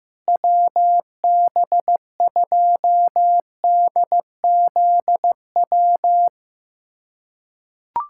Morse code, heard as a keyed tone, about 15 wpm, 700 Hz. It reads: WB2DZW
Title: USA Callsign DE Callsign - 6 Characters - 15wpm